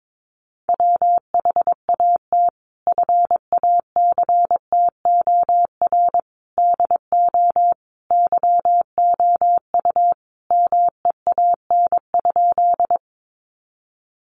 Morse code, heard as a keyed tone, about 22 wpm, 700 Hz. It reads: WHAT FACTOR DO YOU MEAN?
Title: Sentences from Top 1300 Words 22wpm